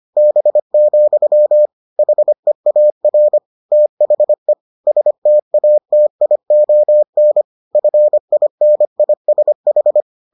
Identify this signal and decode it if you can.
Morse code, heard as a keyed tone, about 25 wpm, 600 Hz. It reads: B, HEAR THE STATION FINISH